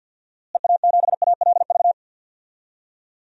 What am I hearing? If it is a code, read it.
Morse code, heard as a keyed tone, about 52 wpm, 700 Hz. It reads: EA7ALV
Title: DXCC-100 Call Signs - 6 Characters - Repeated 3x - 50wpm